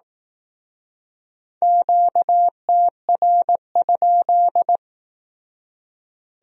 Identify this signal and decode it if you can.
Morse code, heard as a keyed tone, about 18 wpm, 700 Hz. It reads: QTR?
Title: Speed-Racing Q-Codes - 15wpm